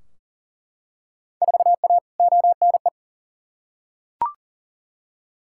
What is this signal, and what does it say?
Morse code, heard as a keyed tone, about 40 wpm, 700 Hz. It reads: 4A ONE
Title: Field Day Exchanges Encore - 40wpm